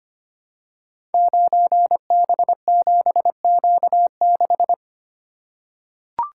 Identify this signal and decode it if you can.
Morse code, heard as a keyed tone, about 25 wpm, 700 Hz. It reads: 9B7Q6
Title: Code Groups - 5 Letter:Numbers - 25wpm